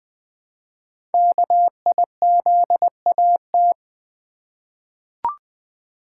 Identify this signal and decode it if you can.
Morse code, heard as a keyed tone, about 20 wpm, 700 Hz. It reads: KIZAT
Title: Code Groups - 5 Letters - 20wpm